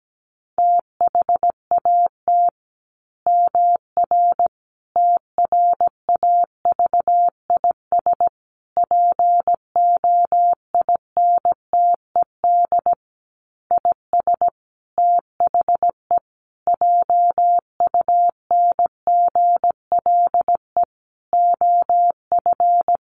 Morse code, heard as a keyed tone, about 17 wpm, 700 Hz. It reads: THAT MR TRAVIS POINTED IS THE JUNGLE OF
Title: A Sound Of Thunder 17wpm